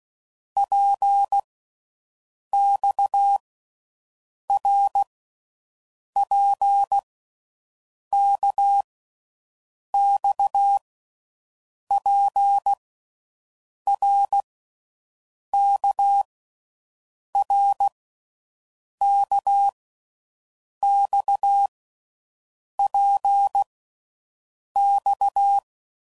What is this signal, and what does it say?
Morse code, heard as a keyed tone, about 16 wpm, 800 Hz. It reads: P X R P K X P R K R K X P X
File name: Groups / Basic Letter Groups / RKPX